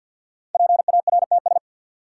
Morse code, heard as a keyed tone, about 53 wpm, 700 Hz. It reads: PARTS
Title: Sets of 3 Words - New Word Drill - 50wpm